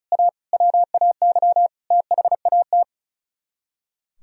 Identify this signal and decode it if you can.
Morse code, heard as a keyed tone, about 45 wpm, 700 Hz. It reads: A WAY THAT